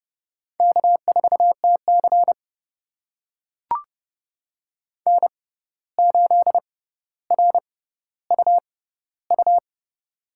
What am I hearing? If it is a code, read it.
Morse code, heard as a keyed tone, about 29 wpm, spaced as for 14 wpm, 700 Hz. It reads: K4TC N8RUU
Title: USA Call Signs - 6x Character Spacing - 30wpm